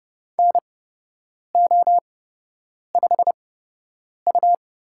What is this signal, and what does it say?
Morse code, heard as a keyed tone, about 30 wpm, spaced as for 8 wpm, 700 Hz. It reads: NO5U
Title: USA Call Signs - 4 Characters - 8x Character Spacing - 30wpm